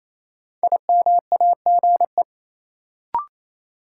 Morse code, heard as a keyed tone, about 28 wpm, 700 Hz. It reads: IMAGE
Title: Sentences from Top 700 Words - New Word Drill - 28wpm